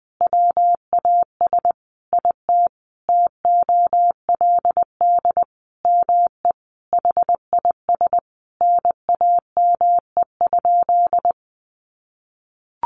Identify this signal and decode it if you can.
Morse code, heard as a keyed tone, about 20 wpm, 700 Hz. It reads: WAS IT TOLD ME HIS NAME?
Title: Sentences from Top 400 Words 20wpm